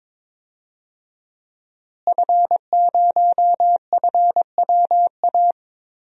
Morse code, heard as a keyed tone, about 21 wpm, 700 Hz. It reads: F0FWA
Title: DXCC-100 Call Signs - 5 Characters - Repeated 2x - 22wpm